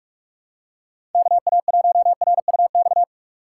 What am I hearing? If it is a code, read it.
Morse code, heard as a keyed tone, about 45 wpm, 700 Hz. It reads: KA1RUX
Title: USA Call Signs - 6 Characters - 2x Character Spacing - 45wpm